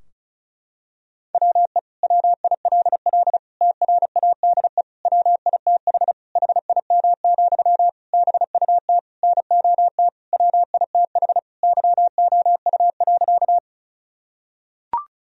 Morse code, heard as a keyed tone, about 34 wpm, 700 Hz. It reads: WE WILL TRADE WITH HIM, BUT NOT WITH YOU.
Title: Sentences from Top 1000 Words - Review Part 4 - 35wpm